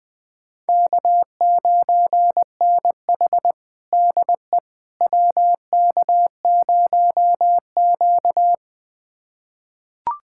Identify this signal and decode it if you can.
Morse code, heard as a keyed tone, about 20 wpm, 700 Hz. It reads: K9NH DE WK0Q
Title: USA Callsign DE Callsign - 4 Characters - 20wpm